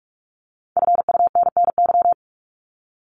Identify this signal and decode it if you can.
Morse code, heard as a keyed tone, about 44 wpm, 700 Hz. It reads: FUNNY
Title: Top 1751-2000 Words 45wpm